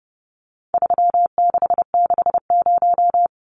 Morse code, heard as a keyed tone, about 30 wpm, 700 Hz. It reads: 3660
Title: Numbers - 4-Digit 30wpm